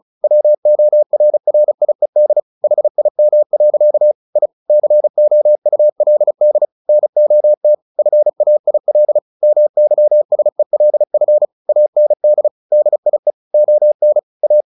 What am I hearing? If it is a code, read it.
Morse code, heard as a keyed tone, about 35 wpm, 600 Hz. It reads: WORRIED HIM. I COULD NOT FAIL MYSELF AND DIE ON A
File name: the old man and the sea 2 35wpm